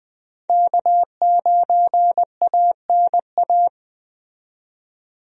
Morse code, heard as a keyed tone, about 20 wpm, 700 Hz. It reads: K9ANA
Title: USA Call Signs - 5 Characters - 20wpm